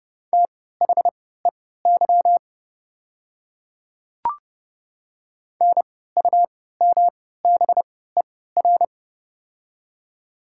Morse code, heard as a keyed tone, about 30 wpm, spaced as for 17 wpm, 700 Hz. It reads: THEY NUMBER
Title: 3x Character Spacing - Top 300 Words - 30wpm